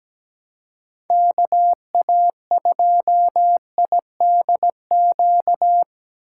Morse code, heard as a keyed tone, about 17 wpm, 700 Hz. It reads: KA2IDQ
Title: USA Call Signs - 6 Characters - 3x Character Spacing - 17wpm